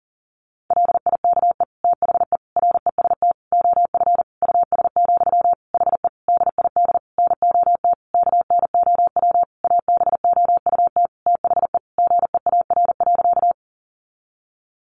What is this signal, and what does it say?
Morse code, heard as a keyed tone, about 40 wpm, 700 Hz. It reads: LIKE THE REST OF US, HE DID NOT KNOW ABOUT THE GEAR.
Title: Sentences from Top 1400 Words 40wpm